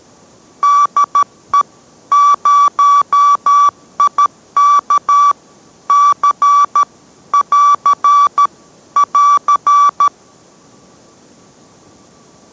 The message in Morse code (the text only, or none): DE 0IK C <AR> <AR>